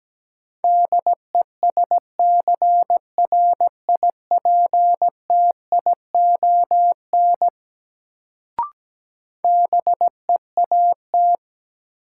Morse code, heard as a keyed tone, about 17 wpm, 700 Hz. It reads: DESCRIPTION BEAT
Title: Sentences from Top 1300 Words - New Word Drill Alt - 17wpm